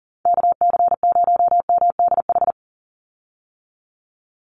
Morse code, heard as a keyed tone, about 40 wpm, 700 Hz. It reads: KC0MDH
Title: USA Call Signs 40wpm